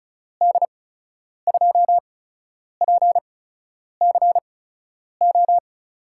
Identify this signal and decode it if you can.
Morse code, heard as a keyed tone, about 34 wpm, spaced as for 10 wpm, 700 Hz. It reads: D2PCO
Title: USA Call Signs - 6 Characters - 8x Character Spacing - 35wpm